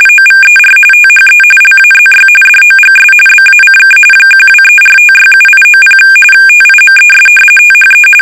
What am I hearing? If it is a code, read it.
RTTY, about 45.45 baud. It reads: OVER THE LAZY DOGS BACK 1234567890 TIMES THE
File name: rtty / RTTY Test 850